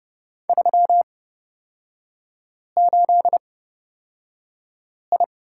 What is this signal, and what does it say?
Morse code, heard as a keyed tone, about 30 wpm, spaced as for 5 wpm, 700 Hz. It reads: 38I
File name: Lesson 36e - BT - Three Characters 30wpm